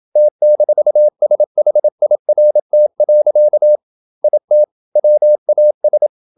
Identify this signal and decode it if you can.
Morse code, heard as a keyed tone, about 27 wpm, 600 Hz. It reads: T-SHIRT. IT WAS
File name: Never 0027wpm